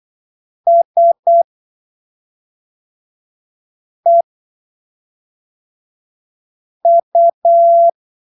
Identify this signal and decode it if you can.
Morse code, heard as a keyed tone, about 8 wpm, spaced as for 3 wpm, 680 Hz. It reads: SEU
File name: arquivos / texto / texto 20 08ppm